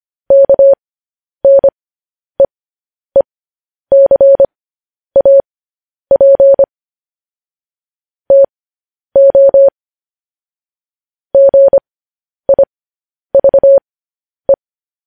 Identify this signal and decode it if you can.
Morse code, heard as a keyed tone, about 25 wpm, spaced as for 10 wpm, 550 Hz. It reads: KNEECAP TO GIVE
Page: news headlines in morse code at 10 wpm